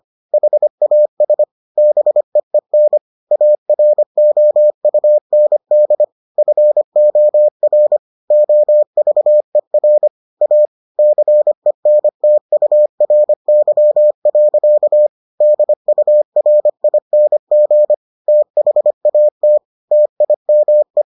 Morse code, heard as a keyed tone, about 25 wpm, 600 Hz. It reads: HAS BEEN AROUND FOR OVER A CENTURY. DURING THAT TIME